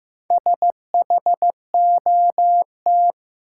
Morse code, heard as a keyed tone, about 16 wpm, 700 Hz. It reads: SHOT
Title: Sentences from Top 1100 Words - New Word Drill - 15wpm